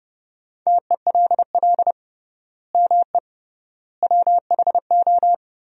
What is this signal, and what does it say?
Morse code, heard as a keyed tone, about 30 wpm, 700 Hz. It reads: TELL ME WHO
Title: 3x Word-Spacing - Sentences from Top 200 Words - 30wpm